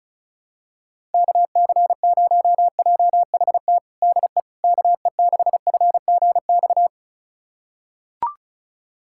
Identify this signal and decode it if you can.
Morse code, heard as a keyed tone, about 36 wpm, 700 Hz. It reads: KC0JHT DE KE6FGX
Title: USA Callsign DE Callsign - 6 Characters - 35wpm